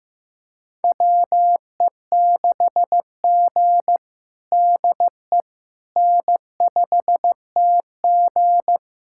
Morse code, heard as a keyed tone, about 16 wpm, 700 Hz. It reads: WE6G DE N5TG
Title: USA Callsign DE Callsign - 4 Characters - 15wpm